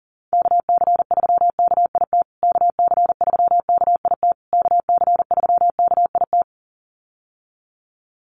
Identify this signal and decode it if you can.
Morse code, heard as a keyed tone, about 37 wpm, 700 Hz. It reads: KC3KIT KC3KIT KC3KIT
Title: USA Call Signs Repeated 3x 40wpm